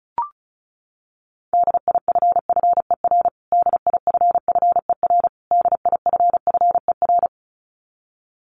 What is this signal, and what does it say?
Morse code, heard as a keyed tone, about 34 wpm, 700 Hz. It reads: DIFFER DIFFER DIFFER
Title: Top 200 Words Repeated 35wpm